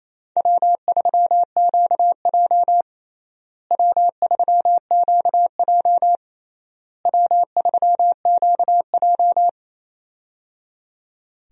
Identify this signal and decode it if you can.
Morse code, heard as a keyed tone, about 28 wpm, 700 Hz. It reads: W3QJ W3QJ W3QJ
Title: USA Call Signs - 4 Characters - Repeated 3x - 28wpm